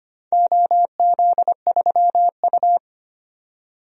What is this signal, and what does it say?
Morse code, heard as a keyed tone, about 25 wpm, 700 Hz. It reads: OZ3U
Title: DXCC-100 Call Signs - 4 Characters - 25wpm